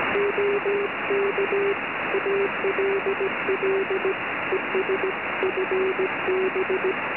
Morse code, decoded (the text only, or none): OK ALLES FB